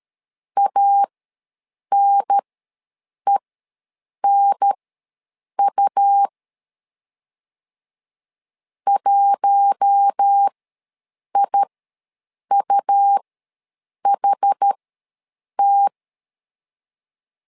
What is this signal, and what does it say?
Morse code, heard as a keyed tone, about 13 wpm, spaced as for 7 wpm, 800 Hz. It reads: ANENU 1IUHT
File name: LEZIONI / 4 Q1S ws 12wpm 1827